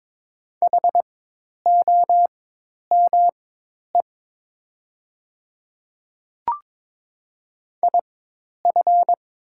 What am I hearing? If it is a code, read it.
Morse code, heard as a keyed tone, about 22 wpm, spaced as for 10 wpm, 700 Hz. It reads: HOME IF
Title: 4x Character Spacing - Top 200 Words - 22wpm